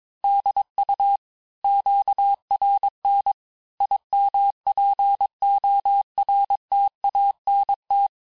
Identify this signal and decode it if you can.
Morse code, heard as a keyed tone, about 22 wpm, 800 Hz. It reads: DU QRN IMPORTANT
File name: cours-du-mercredi / audios / cours-nr-860-FIN-de-QSOr-vitesse-21MPM